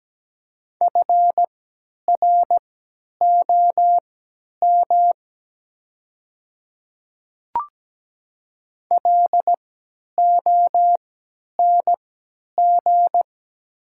Morse code, heard as a keyed tone, about 17 wpm, spaced as for 10 wpm, 700 Hz. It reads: FROM LONG
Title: 3x Character Spacing - Top 200 Words - 17wpm